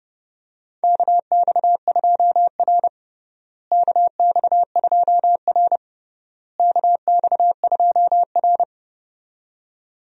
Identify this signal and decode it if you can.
Morse code, heard as a keyed tone, about 30 wpm, 700 Hz. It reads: KX2R KX2R KX2R
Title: USA Call Signs - 4 Characters - Repeated 3x - 30wpm